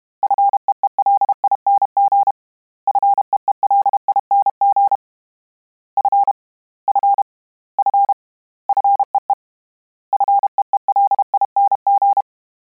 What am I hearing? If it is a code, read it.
Morse code, heard as a keyed tone, about 32 wpm, 800 Hz. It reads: FEELING FEELING F F F FEE FEELING
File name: Combinations / Combinations-mp3s / Combinations@30